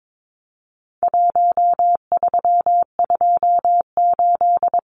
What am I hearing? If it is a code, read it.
Morse code, heard as a keyed tone, about 22 wpm, 700 Hz. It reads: 1328
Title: Numbers - 4-Digit 22wpm